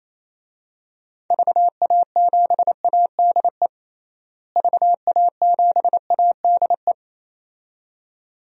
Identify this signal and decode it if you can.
Morse code, heard as a keyed tone, about 28 wpm, 700 Hz. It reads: VA7ADE VA7ADE
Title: DXCC-100 Call Signs - 6 Characters - Repeated 2x - 28wpm